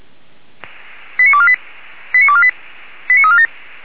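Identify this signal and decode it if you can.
Five-tone selective call (ZVEI), heard as quick runs of five tones, tones of about 70 ms. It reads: 89248 89248 89347
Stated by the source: ZVEI five-tone call received by SDR.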